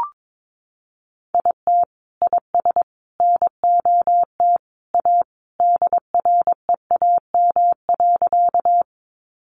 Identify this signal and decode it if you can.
Morse code, heard as a keyed tone, about 22 wpm, 700 Hz. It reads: IT IS NOT A DREAM.
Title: Sentences from Top 1500 Words 22wpm